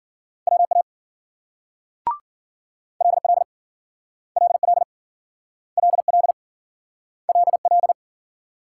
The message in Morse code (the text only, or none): WA LL LL LL LL